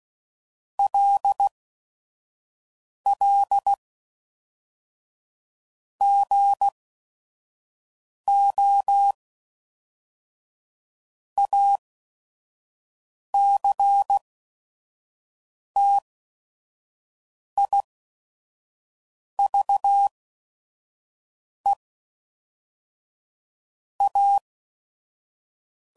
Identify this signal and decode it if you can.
Morse code, heard as a keyed tone, about 16 wpm, spaced as for 5 wpm, 800 Hz. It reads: LL GO ACTIVE A
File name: CW Code Groups @5WPM / Technical Subjects / OR Gate